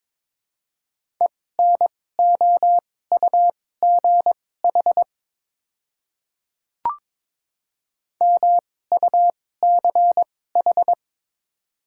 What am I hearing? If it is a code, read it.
Morse code, heard as a keyed tone, about 22 wpm, spaced as for 16 wpm, 700 Hz. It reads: ENOUGH MUCH
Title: 2x Character Spacing - Top 500 Words - 22wpm